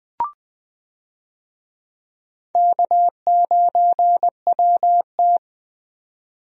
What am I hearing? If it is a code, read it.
Morse code, heard as a keyed tone, about 20 wpm, 700 Hz. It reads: K9WT
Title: USA Call Signs - 4 Characters - 20wpm